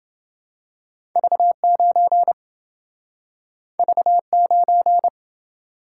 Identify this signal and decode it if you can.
Morse code, heard as a keyed tone, about 27 wpm, 700 Hz. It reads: V9 V9